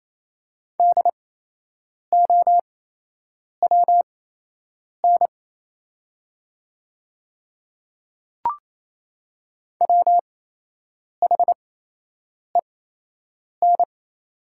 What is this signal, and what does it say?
Morse code, heard as a keyed tone, about 28 wpm, spaced as for 8 wpm, 700 Hz. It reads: DOWN WHEN
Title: 8x Character Spacing - Top 100 Words - 28wpm